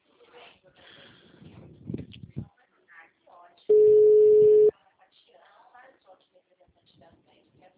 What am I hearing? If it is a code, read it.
Ringback tone (Europe, 425 Hz), with speech.